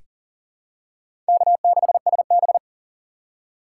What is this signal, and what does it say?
Morse code, heard as a keyed tone, about 40 wpm, 700 Hz. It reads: K6SB
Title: USA Call Signs - 4 Characters - 2x Character Spacing - 40wpm